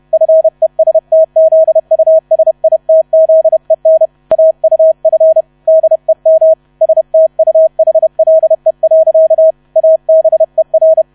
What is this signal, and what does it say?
Morse code, heard as a keyed tone, about 31 wpm, 640 Hz. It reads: FESTZUSITZEN AUF DEM STUHLE. ABER
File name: 30wpm